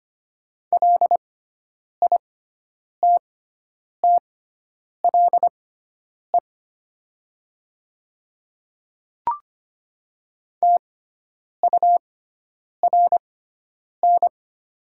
Morse code, heard as a keyed tone, about 25 wpm, spaced as for 9 wpm, 700 Hz. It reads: LITTLE TURN
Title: 6x Character Spacing - Top 200 Words - 25wpm